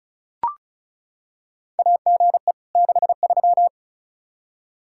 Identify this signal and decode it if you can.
Morse code, heard as a keyed tone, about 34 wpm, 700 Hz. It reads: AGE 63
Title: Practice QSO - Exchange 3 - More Info - 35wpm